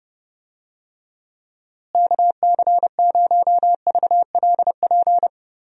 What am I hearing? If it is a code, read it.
Morse code, heard as a keyed tone, about 32 wpm, 700 Hz. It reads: KC0VLP
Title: USA Call Signs - 6 Characters - Repeated 2x - 30wpm